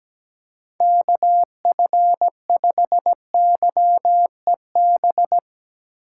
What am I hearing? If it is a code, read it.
Morse code, heard as a keyed tone, about 17 wpm, 700 Hz. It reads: KF5YEB
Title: USA Call Signs - 6 Characters - 2x Character Spacing - 17wpm